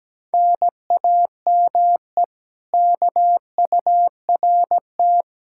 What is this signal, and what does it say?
Morse code, heard as a keyed tone, about 17 wpm, 700 Hz. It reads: NAME KURT
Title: Speed-Racing Common QSO Phrases - 17wpm